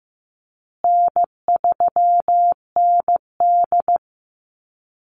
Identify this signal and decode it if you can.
Morse code, heard as a keyed tone, about 15 wpm, 700 Hz. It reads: N3ND
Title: Easy Call Signs Repeated 3x - 15wpm